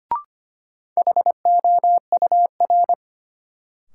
Morse code, heard as a keyed tone, about 25 wpm, 700 Hz. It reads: HOUR
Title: Sentences from Top 400 Words - New Word Drill - 25wpm